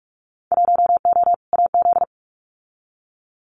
Morse code, heard as a keyed tone, about 44 wpm, 700 Hz. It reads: 1O AZ
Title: Winter Field Day Exchanges 45wpm